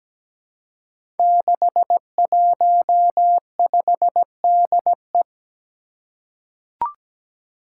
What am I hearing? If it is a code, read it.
Morse code, heard as a keyed tone, about 17 wpm, 700 Hz. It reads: 615DE
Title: Five Challenging Letter-Number Combinations 17wpm